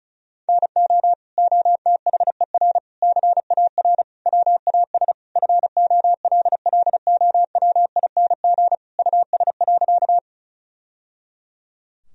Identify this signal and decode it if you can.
Morse code, heard as a keyed tone, about 36 wpm, 700 Hz. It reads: NO OTHER CAR WAS FOLLOWING US.
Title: Sentences from Top 1000 Words - Review Part 4 - 35wpm